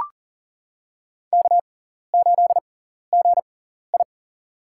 Morse code, heard as a keyed tone, about 40 wpm, spaced as for 14 wpm, 700 Hz. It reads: K8GI
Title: USA Call Signs - 6x Character Spacing - 40wpm